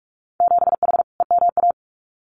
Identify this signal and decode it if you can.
Morse code, heard as a keyed tone, about 46 wpm, 700 Hz. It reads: 7H EMA
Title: Winter Field Day Exchanges Encore 45wpm